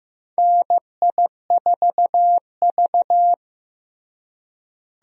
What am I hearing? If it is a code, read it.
Morse code, heard as a keyed tone, about 16 wpm, 700 Hz. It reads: NI4V
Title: USA Call Signs - 4 Characters - Repeated 2x - 15wpm